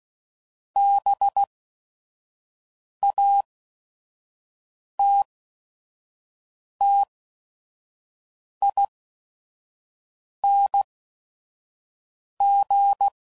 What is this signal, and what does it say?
Morse code, heard as a keyed tone, about 16 wpm, spaced as for 5 wpm, 800 Hz. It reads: BATTING
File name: Suffixes / Special ING's / VerySlowToFaster ING's